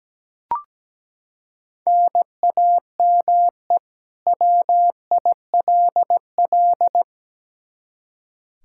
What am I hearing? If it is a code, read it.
Morse code, heard as a keyed tone, about 17 wpm, 700 Hz. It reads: NAME WILL
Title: K1USN SST QSO Practice 17wpm